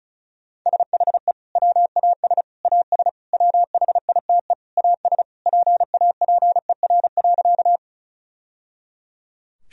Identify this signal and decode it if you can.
Morse code, heard as a keyed tone, about 36 wpm, 700 Hz. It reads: SHE WAS AS WHITE AS PAPER.
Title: Sentences from Top 1000 Words - Review Part 1 - 35wpm